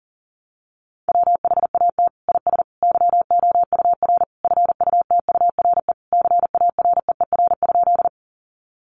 Morse code, heard as a keyed tone, about 40 wpm, 700 Hz. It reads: WHAT IS YOUR FUTURE CAREER?
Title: Sentences from Top 1000 Words 40wpm